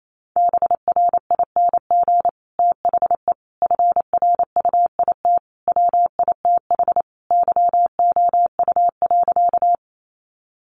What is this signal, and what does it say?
Morse code, heard as a keyed tone, about 28 wpm, 700 Hz. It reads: BRING THE FRUIT WITH YOU.
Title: Sentences from Top 1300 Words 28wpm